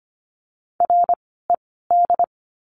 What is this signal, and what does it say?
Morse code, heard as a keyed tone, about 25 wpm, spaced as for 16 wpm, 700 Hz. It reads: RED
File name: course / 25:10wpm / 094 Lesson 16f - Letter c - Words 25wpm